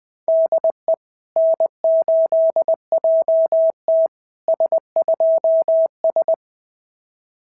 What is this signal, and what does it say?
Morse code, heard as a keyed tone, about 20 wpm, 690 Hz. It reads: DE N8JT S2S
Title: SOTA QSO Practice 20wpm